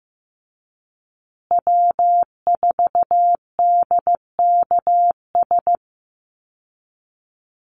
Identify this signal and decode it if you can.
Morse code, heard as a keyed tone, about 15 wpm, 700 Hz. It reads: W4DKS
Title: Call Signs Repeated 3x - 15wpm